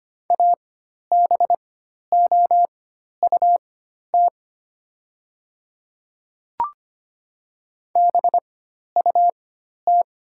Morse code, heard as a keyed tone, about 25 wpm, spaced as for 12 wpm, 700 Hz. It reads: ABOUT BUT
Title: 4x Character Spacing - Top 200 Words - 25wpm